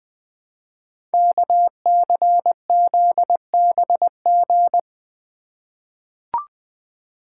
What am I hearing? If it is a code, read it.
Morse code, heard as a keyed tone, about 20 wpm, 700 Hz. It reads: KCZBG
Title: Code Groups - 5 Letters - 20wpm